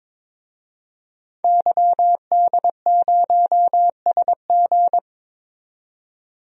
Morse code, heard as a keyed tone, about 22 wpm, 700 Hz. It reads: YD0SG
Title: DXCC-100 Call Signs - 5 Characters - 22wpm